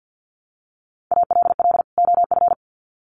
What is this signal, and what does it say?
Morse code, heard as a keyed tone, about 48 wpm, 700 Hz. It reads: ALL OF
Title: Sets of 2 Words 50wpm